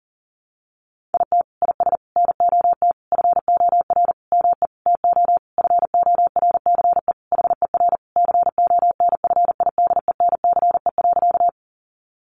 Morse code, heard as a keyed tone, about 40 wpm, 700 Hz. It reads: IT IS NOT FOR ME TO FORCE HER CONFIDENCE.
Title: Sentences from Top 1500 Words 40wpm